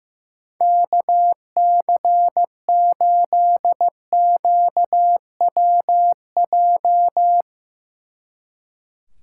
Morse code, heard as a keyed tone, about 16 wpm, 700 Hz. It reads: KC8QWJ
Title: Speed Racing - USA 2x3 Call Signs 15wpm